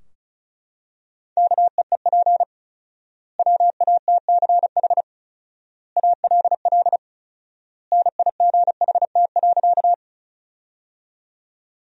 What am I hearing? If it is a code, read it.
Morse code, heard as a keyed tone, about 34 wpm, 700 Hz. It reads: KEEP WATCH ALL NIGHT.
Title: 4x Word-Spacing - Sentences from Top 300 Words - 35wpm